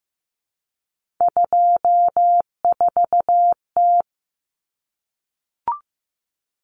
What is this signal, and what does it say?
Morse code, heard as a keyed tone, about 15 wpm, 700 Hz. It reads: 24T
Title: Common Cut Numbers - 3-Digit 15wpm